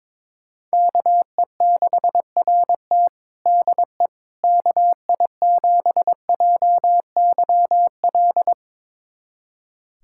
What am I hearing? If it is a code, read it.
Morse code, heard as a keyed tone, about 22 wpm, 700 Hz. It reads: KE6RT DE KI7JYL
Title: USA Callsign DE Callsign 22wpm